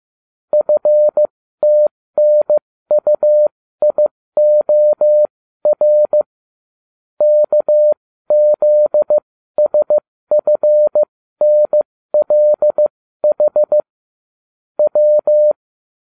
Morse code, heard as a keyed tone, about 15 wpm, spaced as for 13 wpm, 600 Hz. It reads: FTNUIOR KZSFNLH W
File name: cw / ubacw34 3